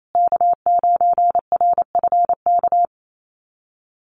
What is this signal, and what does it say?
Morse code, heard as a keyed tone, about 28 wpm, 700 Hz. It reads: K9RFK
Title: USA Call Signs Repeated 3x 28wpm